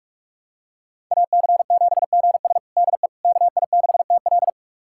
Morse code, heard as a keyed tone, about 42 wpm, 700 Hz. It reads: AC7GS DE KI6TL
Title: USA Callsign DE Callsign - 5 Characters - 45wpm